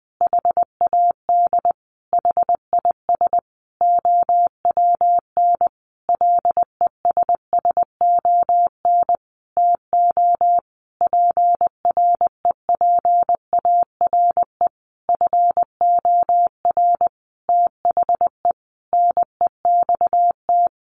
Morse code, heard as a keyed tone, about 20 wpm, 700 Hz. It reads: HAD HIS OWN LESSON TO PREPARE FOR THE NEXT
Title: Sentences from Top 1200 Words 20wpm